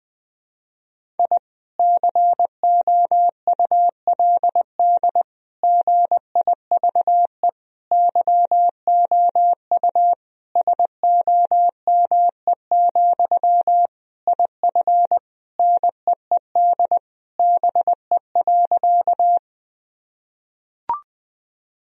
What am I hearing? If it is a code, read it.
Morse code, heard as a keyed tone, about 20 wpm, 700 Hz. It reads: I COULD GIVE YOU SOME, IF NEED BE.
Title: Sentences from Top 1000 Words - Review Part 1 - 20wpm